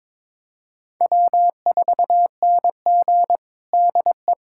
Morse code, heard as a keyed tone, about 21 wpm, 700 Hz. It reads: W4NG DE
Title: USA Callsign DE Callsign - 4 Characters - 22wpm